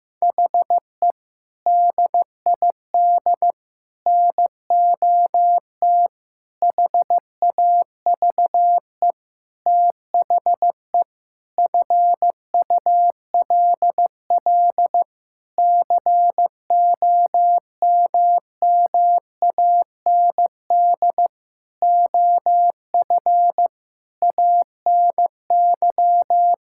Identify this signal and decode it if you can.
Morse code, heard as a keyed tone, about 15 wpm, 700 Hz. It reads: HE DID NOT HAVE THE FULL COMMAND OF ANY
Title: Sentences from Top 1600 Words 15wpm